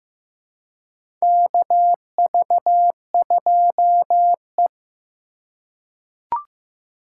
Morse code, heard as a keyed tone, about 15 wpm, 700 Hz. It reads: KV2E
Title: USA Call Signs - 4 Characters - 2x Character Spacing - 15wpm